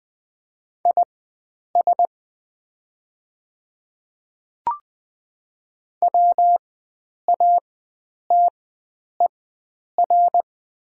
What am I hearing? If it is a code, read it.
Morse code, heard as a keyed tone, about 20 wpm, spaced as for 9 wpm, 700 Hz. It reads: IS WATER
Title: 4x Character Spacing - Top 300 Words - 20wpm